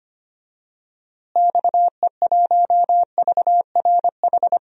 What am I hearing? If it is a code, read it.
Morse code, heard as a keyed tone, about 25 wpm, 700 Hz. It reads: XE1VRH
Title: DXCC-100 Call Signs - 6 Characters - Repeated 3x - 25wpm